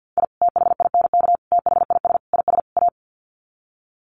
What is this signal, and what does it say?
Morse code, heard as a keyed tone, about 50 wpm, 700 Hz. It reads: I THINK THIS IS A
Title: Sets of 5 Words 50wpm